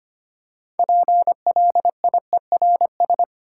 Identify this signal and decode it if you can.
Morse code, heard as a keyed tone, about 26 wpm, 700 Hz. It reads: PLIERS
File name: practice / 25wpm / Easy Words with Hints - 25wpm